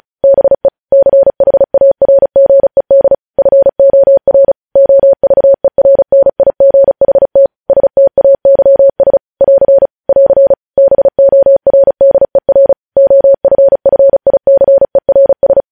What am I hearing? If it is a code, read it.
Morse code, heard as a keyed tone, about 36 wpm, 550 Hz. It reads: BE CHARGED FOR OVERNIGHT STAYS <AR> <AR> BORDER OFFICERS
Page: News headlines in Morse code at 35 wpm